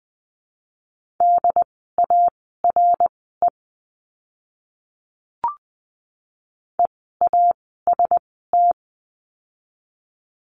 Morse code, heard as a keyed tone, about 20 wpm, spaced as for 14 wpm, 700 Hz. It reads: DARE EAST
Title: 068 Lesson 12f - Letter d - Words 20wpm